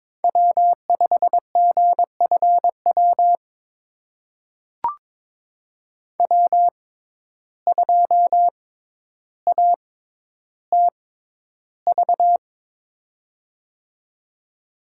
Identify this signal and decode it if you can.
Morse code, heard as a keyed tone, about 21 wpm, spaced as for 11 wpm, 700 Hz. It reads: W5GFW W2ATV
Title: USA Call Signs - 5 Characters - 6x Character Spacing - 22wpm